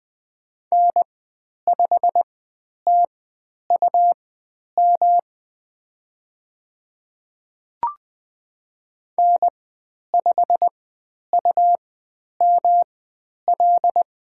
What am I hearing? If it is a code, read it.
Morse code, heard as a keyed tone, about 20 wpm, spaced as for 10 wpm, 700 Hz. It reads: N5TUM N5UML
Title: 109 Lesson 18g - Letter m - Call Signs 20wpm